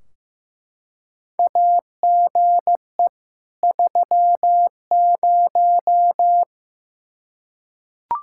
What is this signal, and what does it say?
Morse code, heard as a keyed tone, about 16 wpm, 700 Hz. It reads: AGE 30
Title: Practice QSO - Exchange 3 - More Info - 15wpm